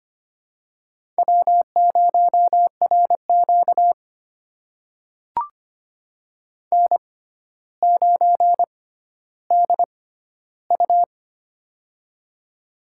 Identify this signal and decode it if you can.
Morse code, heard as a keyed tone, about 25 wpm, spaced as for 13 wpm, 700 Hz. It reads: W0RQ N9DU
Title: USA Call Signs - 4 Characters - 6x Character Spacing - 25wpm